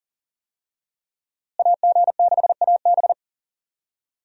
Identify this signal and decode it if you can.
Morse code, heard as a keyed tone, about 40 wpm, 700 Hz. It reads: AG6AB
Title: USA Call Signs - 5 Characters - Repeated 2x - 40wpm